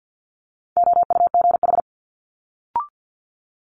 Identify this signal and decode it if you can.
Morse code, heard as a keyed tone, about 50 wpm, 700 Hz. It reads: OUGH
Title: Multi-letter Phonemes Quick Fire 50wpm